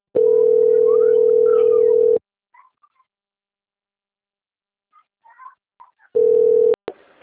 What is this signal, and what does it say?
Ringback tone (North America), 440 + 480 Hz.
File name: out- - - - - .9466